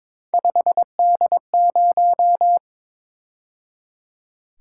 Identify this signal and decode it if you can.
Morse code, heard as a keyed tone, about 26 wpm, 700 Hz. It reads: 5D0
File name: Speed-Racing - Three Letter-Number Combinations - 22wpm